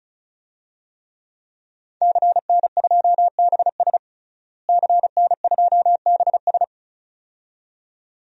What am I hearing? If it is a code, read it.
Morse code, heard as a keyed tone, about 34 wpm, 700 Hz. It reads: CN2BS CN2BS
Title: DXCC-100 Call Signs - 5 Characters - Repeated 2x - 35wpm